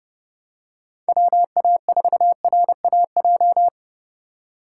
Morse code, heard as a keyed tone, about 28 wpm, 700 Hz. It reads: WA4RAJ